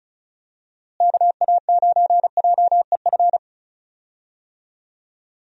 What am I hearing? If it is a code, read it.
Morse code, heard as a keyed tone, about 34 wpm, 700 Hz. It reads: KA9JEF
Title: USA Call Signs - 6 Characters - Repeated 2x - 35wpm